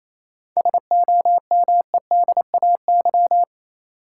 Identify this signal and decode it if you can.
Morse code, heard as a keyed tone, about 28 wpm, 700 Hz. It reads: SOMEDAY
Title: Sentences from Top 1600 Words - New Word Drill - 28wpm